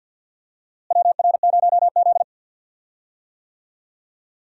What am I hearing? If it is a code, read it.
Morse code, heard as a keyed tone, about 46 wpm, 700 Hz. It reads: WR0Z
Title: USA Call Signs - 4 Characters - Repeated 2x - 50wpm